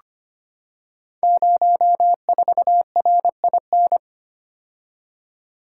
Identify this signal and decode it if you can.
Morse code, heard as a keyed tone, about 25 wpm, 700 Hz. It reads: 04RIN
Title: Code Groups - 5 Letter:Numbers - 25wpm